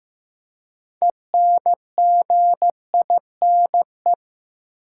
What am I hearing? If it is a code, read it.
Morse code, heard as a keyed tone, about 15 wpm, 700 Hz. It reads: ENGINE
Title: Six Letter Words - 15wpm